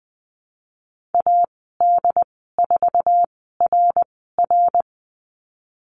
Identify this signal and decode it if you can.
Morse code, heard as a keyed tone, about 19 wpm, spaced as for 14 wpm, 700 Hz. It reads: AD4RR
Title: 076 Lesson 13g - Letter L - Call Signs 20wpm